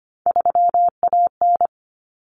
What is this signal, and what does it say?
Morse code, heard as a keyed tone, about 25 wpm, 700 Hz. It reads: 3AN
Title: Common Cut Numbers - 3-Digit 25wpm